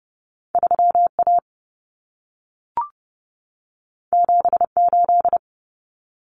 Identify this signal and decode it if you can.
Morse code, heard as a keyed tone, about 30 wpm, 700 Hz. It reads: 3A 78
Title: Common Cut Numbers - 2-Digit 30wpm